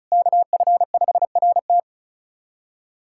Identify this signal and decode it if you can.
Morse code, heard as a keyed tone, about 36 wpm, 700 Hz. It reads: KF5RT
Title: USA Callsign DE Callsign - 5 Characters - 35wpm